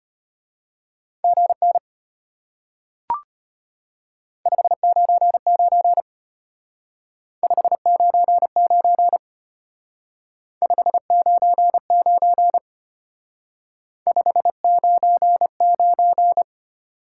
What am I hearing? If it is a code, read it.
Morse code, heard as a keyed tone, about 32 wpm, 700 Hz. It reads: GN 599 599 599 599